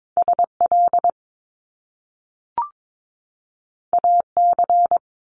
Morse code, heard as a keyed tone, about 22 wpm, 700 Hz. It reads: SL AC
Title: Top Two Letter Combinations in Words 22wpm